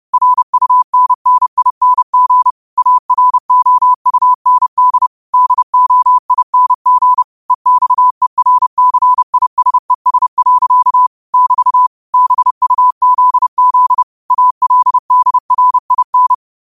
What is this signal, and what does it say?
Morse code, heard as a keyed tone, about 30 wpm, 1000 Hz. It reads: RUNNING AROUND DOING EXERCISES. <BT> BUZZ ALDRIN
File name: qotdfiles / 30wpm